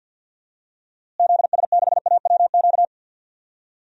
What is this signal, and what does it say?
Morse code, heard as a keyed tone, about 50 wpm, 700 Hz. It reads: ZS6AWQ
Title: DXCC-100 Call Signs - 6 Characters - Repeated 3x - 50wpm